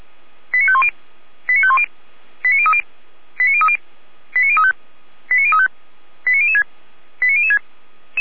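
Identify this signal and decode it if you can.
Five-tone selective call (ZVEI), heard as quick runs of five tones, tones of about 70 ms. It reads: 89510 89510 89030 89030 89035 89035 89006 89006
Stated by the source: ZVEI five-tone selective call 89099, received via SDR, archived by feuerwehr-orlamuende.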